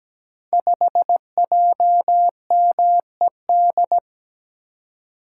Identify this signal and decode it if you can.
Morse code, heard as a keyed tone, about 17 wpm, 700 Hz. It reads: 5JMED
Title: Code Groups - 5 Letter:Numbers - 17wpm